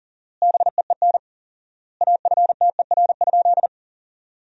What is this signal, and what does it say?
Morse code, heard as a keyed tone, about 40 wpm, 700 Hz. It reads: BEEN AFTER?
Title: 4x Word-Spacing - Sentences from Top 200 Words - 40wpm